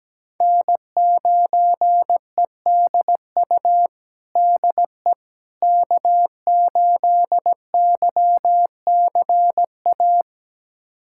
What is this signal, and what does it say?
Morse code, heard as a keyed tone, about 17 wpm, 700 Hz. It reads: N9EDU DE K8YCA
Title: USA Callsign DE Callsign - 5 Characters - 17wpm